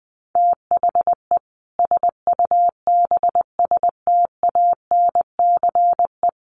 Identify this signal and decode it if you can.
Morse code, heard as a keyed tone, about 19 wpm, 700 Hz. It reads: THE SUBSTANCE
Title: Sentences from Top 1500 Words 20wpm